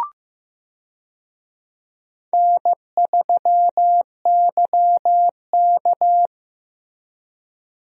Morse code, heard as a keyed tone, about 15 wpm, 700 Hz. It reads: N3YK
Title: USA Call Signs - 4 Characters - Repeated 2x - 15wpm